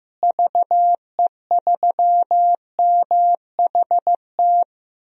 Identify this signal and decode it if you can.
Morse code, heard as a keyed tone, about 15 wpm, 700 Hz. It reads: VE3MHT
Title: DXCC-100 Call Signs - 6 Characters - Repeated 3x - 15wpm